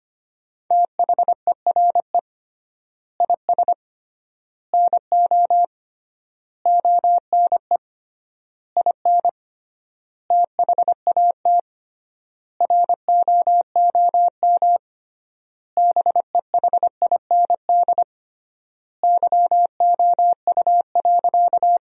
Morse code, heard as a keyed tone, about 25 wpm, 700 Hz. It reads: THERE IS NO ONE IN THAT ROOM BEHIND YOU.
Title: 3x Word-Spacing - Sentences from Top 500 Words - 25wpm